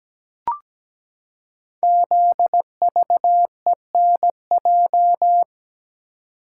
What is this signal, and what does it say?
Morse code, heard as a keyed tone, about 17 wpm, 700 Hz. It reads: ZVENJ
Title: Code Groups - 5 Letters - 17wpm